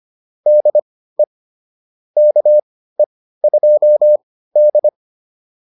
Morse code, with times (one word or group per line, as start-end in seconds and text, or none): 0.5-1.2 DE
2.2-4.9 KE2D